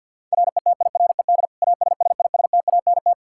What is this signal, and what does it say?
Morse code, heard as a keyed tone, about 49 wpm, 700 Hz. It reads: RETIRED ASSISTANT
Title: Practice QSO - Exchange 3 - More Info - 50wpm